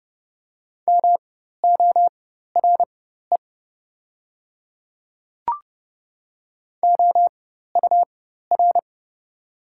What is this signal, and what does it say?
Morse code, heard as a keyed tone, about 30 wpm, spaced as for 14 wpm, 690 Hz. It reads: MORE OUR
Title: 4x Character Spacing - Top 200 Words - 30wpm